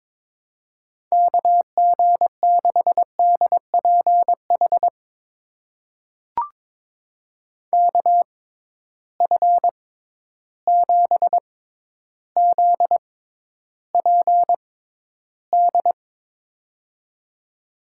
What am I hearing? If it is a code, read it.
Morse code, heard as a keyed tone, about 22 wpm, spaced as for 11 wpm, 700 Hz. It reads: KG6DPH KF7ZPD
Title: USA Call Signs - 6 Characters - 6x Character Spacing - 22wpm